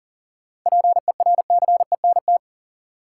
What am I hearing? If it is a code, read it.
Morse code, heard as a keyed tone, about 40 wpm, 700 Hz. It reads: PERCENT
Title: Sets of 2 Words - New Word Drill - 40wpm